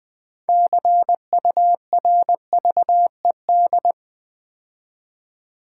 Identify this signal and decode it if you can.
Morse code, heard as a keyed tone, about 20 wpm, 700 Hz. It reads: CURVED
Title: 6 Letter Words - First 3 Letters Given - 20wpm